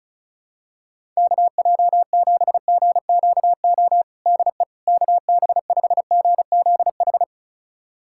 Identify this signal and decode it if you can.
Morse code, heard as a keyed tone, about 36 wpm, 700 Hz. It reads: KJ7GQO DE KB5GZH
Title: USA Callsign DE Callsign - 6 Characters - 35wpm